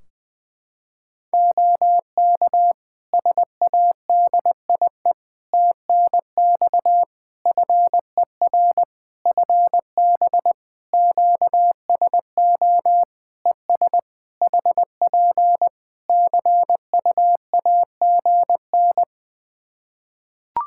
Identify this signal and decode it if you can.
Morse code, heard as a keyed tone, about 20 wpm, 700 Hz. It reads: OK SADIE TNX FER FB QSO ES HP CUAGN
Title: Practice QSO - Exchange 4 - Ending - 20wpm